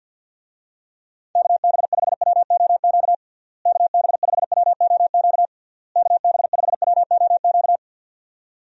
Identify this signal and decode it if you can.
Morse code, heard as a keyed tone, about 45 wpm, 700 Hz. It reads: KB5WOQ KB5WOQ KB5WOQ
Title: USA Call Signs - 6 Characters - Repeated 3x - 50wpm